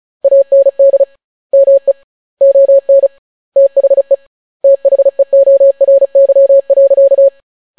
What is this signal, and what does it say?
Morse code, heard as a keyed tone, about 36 wpm, 550 Hz. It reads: AND ME ON THE THEORY.